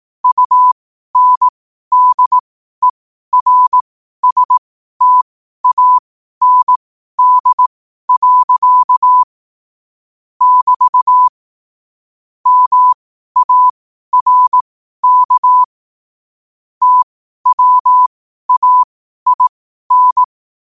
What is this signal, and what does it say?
Morse code, heard as a keyed tone, about 18 wpm, spaced as for 12 wpm, 1000 Hz. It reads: UNDERSTAND. <BT> MARK TWAIN
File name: qotdfiles / 12wpm